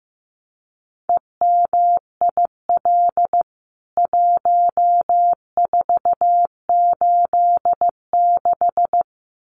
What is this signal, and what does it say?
Morse code, heard as a keyed tone, about 15 wpm, 700 Hz. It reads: EMIL 1486
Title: CWT Exchange 15wpm